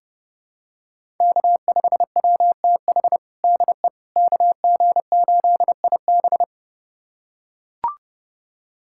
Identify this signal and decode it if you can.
Morse code, heard as a keyed tone, about 32 wpm, 700 Hz. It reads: K5WTH DE KG8IB
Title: USA Callsign DE Callsign - 5 Characters - 30wpm